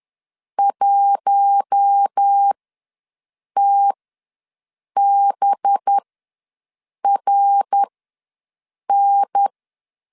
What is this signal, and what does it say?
Morse code, heard as a keyed tone, about 10 wpm, spaced as for 6 wpm, 800 Hz. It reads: 1TBRN
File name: LEZIONI / 5 LBVC / 5 LBVC ws 10wpm 3733